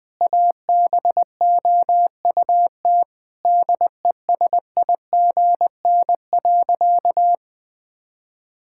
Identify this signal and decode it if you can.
Morse code, heard as a keyed tone, about 21 wpm, 700 Hz. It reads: ABOUT DESIGN.
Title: Sentences from Top 1000 Words - Review Part 5 - 20wpm